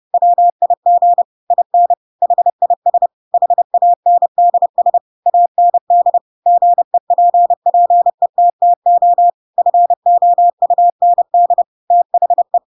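Morse code, heard as a keyed tone, about 30 wpm, 700 Hz. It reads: WIG IN HIS HANDS AND GEPPETTO FOUND THE
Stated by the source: Chapter 2 of The Adventures of Pinocchio read as a book in Morse code (CW) at 30 wpm, by ik1hge.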